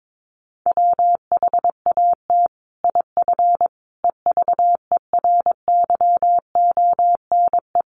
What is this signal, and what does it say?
Morse code, heard as a keyed tone, about 22 wpm, 700 Hz. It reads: WHAT IF EVERYONE
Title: Sentences from Top 1400 Words 22wpm